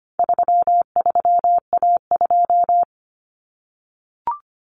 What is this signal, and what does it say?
Morse code, heard as a keyed tone, about 25 wpm, 700 Hz. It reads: 33A2
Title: Common Cut Numbers - 4-Digit 25wpm